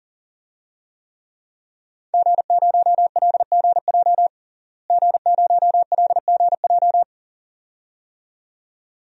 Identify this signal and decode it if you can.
Morse code, heard as a keyed tone, about 37 wpm, 700 Hz. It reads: G0LGJ G0LGJ
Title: DXCC-100 Call Signs - 5 Characters - Repeated 2x - 40wpm